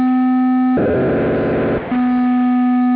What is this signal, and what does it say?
POCSAG pager data.